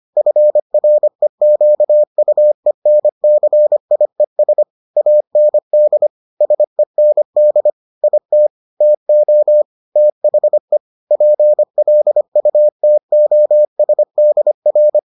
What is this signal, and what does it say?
Morse code, heard as a keyed tone, about 25 wpm, 600 Hz. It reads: FREQUENCIES AND SEND IT TO THE PLUTOSDR